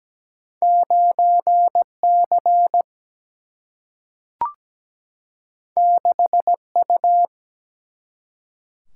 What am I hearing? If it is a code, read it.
Morse code, heard as a keyed tone, about 17 wpm, 700 Hz. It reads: 9C 6U
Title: Single-Letter Single-Number 17wpm